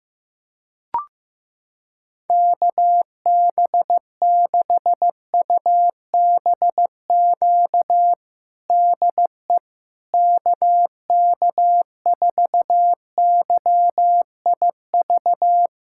Morse code, heard as a keyed tone, about 15 wpm, 700 Hz. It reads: KB6UBQ DE KK4YIV
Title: USA Callsign DE Callsign - 6 Characters - 15wpm